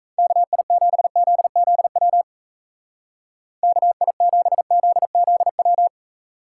Morse code, heard as a keyed tone, about 35 wpm, 700 Hz. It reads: KI7ZZW KI7ZZW
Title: Speed Racing - USA Call Signs 28wpm